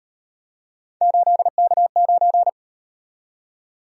Morse code, heard as a keyed tone, about 38 wpm, 700 Hz. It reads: 8K9